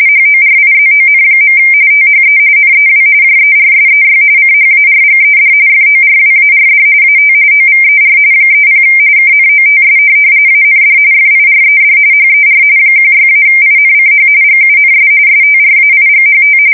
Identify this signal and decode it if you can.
RTTY, about 45.45 baud. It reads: HORIZON. THE NEW JERSEY CHARGED FORWARD AT OVER 30 KNOTS AND FIRED HER 16-INCH GUNS STRAIGHT
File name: fsk170 / USS-New-Jersey